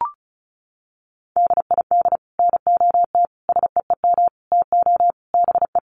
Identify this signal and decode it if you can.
Morse code, heard as a keyed tone, about 34 wpm, 700 Hz. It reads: DID NOT SEEM TO BE
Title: Sets of 5 Words Encore 35wpm